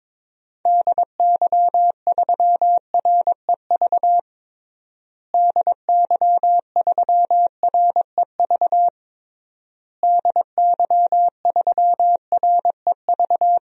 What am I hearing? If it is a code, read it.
Morse code, heard as a keyed tone, about 22 wpm, 700 Hz. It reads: DY3REV DY3REV DY3REV
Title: DXCC-100 Call Signs - 6 Characters - Repeated 3x - 22wpm